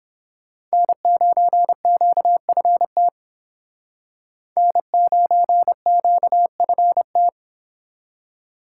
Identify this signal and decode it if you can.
Morse code, heard as a keyed tone, about 31 wpm, 700 Hz. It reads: N9QFT N9QFT